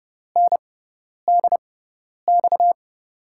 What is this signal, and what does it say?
Morse code, heard as a keyed tone, about 29 wpm, spaced as for 10 wpm, 700 Hz. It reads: NDX
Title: USA Call Signs - 6 Characters - 6x Character Spacing - 30wpm